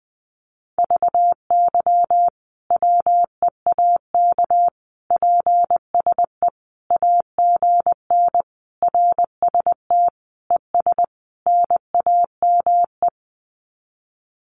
Morse code, heard as a keyed tone, about 20 wpm, 700 Hz. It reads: VY WEAK PSE AGN RST ES NAME
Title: Common QSO Phrases 20wpm